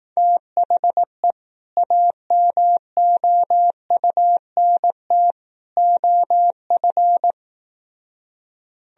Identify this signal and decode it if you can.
Morse code, heard as a keyed tone, about 18 wpm, 700 Hz. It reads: THE AMOUNT OF
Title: Speed-Racing - Sets of 3 Words 15wpm